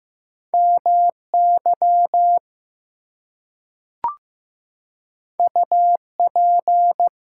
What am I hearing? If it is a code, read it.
Morse code, heard as a keyed tone, about 15 wpm, 700 Hz. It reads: MY UP
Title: Two Letter Words - 15wpm